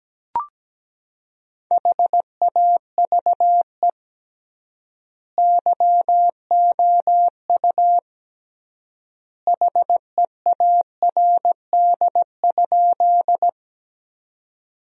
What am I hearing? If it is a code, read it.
Morse code, heard as a keyed tone, about 17 wpm, 700 Hz. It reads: HAVE YOU HEARD?
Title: 3x Word-Spacing - Sentences from Top 400 Words - 17wpm